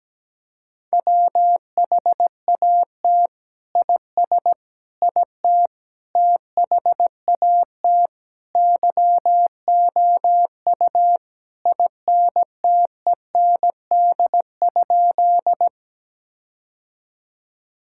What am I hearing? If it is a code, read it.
Morse code, heard as a keyed tone, about 17 wpm, 700 Hz. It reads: WHAT IS IT THAT YOU INTEND?
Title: Sentences from Top 1600 Words 17wpm